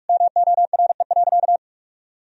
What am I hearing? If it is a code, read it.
Morse code, heard as a keyed tone, about 42 wpm, 700 Hz. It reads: MORE.
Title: Sentences from Top 1000 Words - Review Part 4 - 45wpm